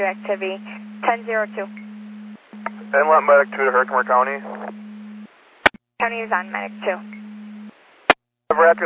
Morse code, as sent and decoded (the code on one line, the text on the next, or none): none